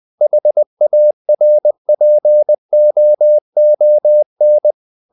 Morse code, heard as a keyed tone, about 20 wpm, 600 Hz. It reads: HARPOON
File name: the old man and the sea part 2 20wpm